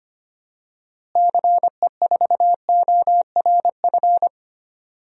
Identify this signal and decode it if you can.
Morse code, heard as a keyed tone, about 25 wpm, 700 Hz. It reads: CE4ORF
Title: DXCC-100 Call Signs - 6 Characters - Repeated 3x - 25wpm